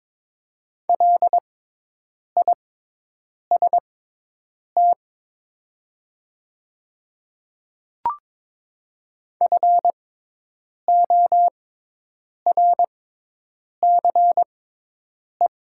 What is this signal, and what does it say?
Morse code, heard as a keyed tone, about 22 wpm, spaced as for 8 wpm, 700 Hz. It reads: LIST FORCE
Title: 6x Character Spacing - Top 500 Words - 22wpm